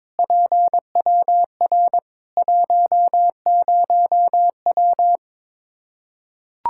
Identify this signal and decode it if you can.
Morse code, heard as a keyed tone, about 21 wpm, 700 Hz. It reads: PWR 10W
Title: Practice QSO - Exchange 2 - Info - 22wpm